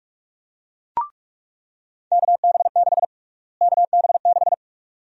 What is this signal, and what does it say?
Morse code, heard as a keyed tone, about 46 wpm, 700 Hz. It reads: KB6 KB6
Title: USA Call Sign Prefixes - Repeated 2x - 45wpm